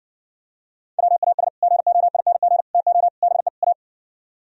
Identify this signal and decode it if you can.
Morse code, heard as a keyed tone, about 59 wpm, 700 Hz. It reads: WAS GOING TO BE A